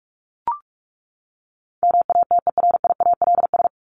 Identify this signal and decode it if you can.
Morse code, heard as a keyed tone, about 44 wpm, 700 Hz. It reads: MATERIALS
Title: ES:S Suffix - Top 100 Words with Base Words 45wpm